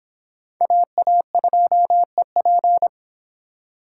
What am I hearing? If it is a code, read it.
Morse code, heard as a keyed tone, about 23 wpm, 700 Hz. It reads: AA2EP